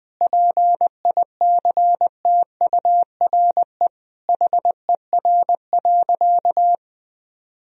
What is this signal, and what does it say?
Morse code, heard as a keyed tone, about 20 wpm, 700 Hz. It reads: PICTURE HER.
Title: Sentences from Top 1000 Words - Review Part 1 - 20wpm